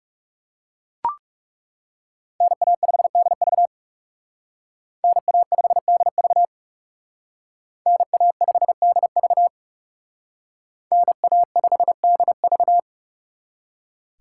Morse code, heard as a keyed tone, about 39 wpm, 700 Hz. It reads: NA5DV NA5DV NA5DV NA5DV